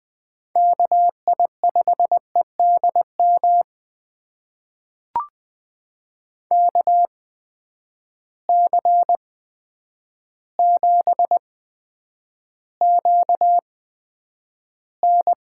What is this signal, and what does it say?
Morse code, heard as a keyed tone, about 21 wpm, spaced as for 9 wpm, 700 Hz. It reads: KI5EDM KC7QN
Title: USA Call Signs - 6 Characters - 8x Character Spacing - 20wpm